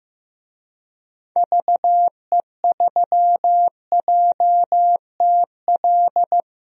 Morse code, heard as a keyed tone, about 15 wpm, 690 Hz. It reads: VE3JTL
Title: DXCC-100 Call Signs - 6 Characters - 15wpm